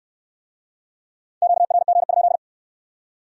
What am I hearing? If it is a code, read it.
Morse code, heard as a keyed tone, about 63 wpm, 700 Hz. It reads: QRG?